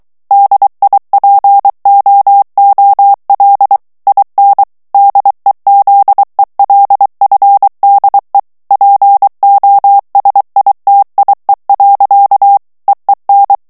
Morse code, heard as a keyed tone, about 23 wpm, 800 Hz. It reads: DIPOOL IN DEZELFDE POSITIE. EEN
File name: HOLLANDAIS texte clair 1 vitesse 22 mots minute